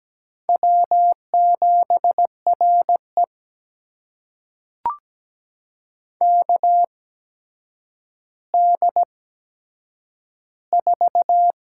Morse code, heard as a keyed tone, about 17 wpm, spaced as for 8 wpm, 700 Hz. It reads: W7RE KD4
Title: USA Call Signs - 8x Character Spacing - 17wpm